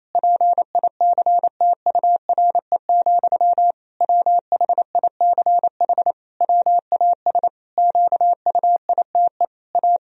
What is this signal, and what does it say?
Morse code, heard as a keyed tone, about 27 wpm, 700 Hz. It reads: PICTURE, WHICH WAS QUITE A
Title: Sentences from Top 1000 Words - Review Part 3 - 28wpm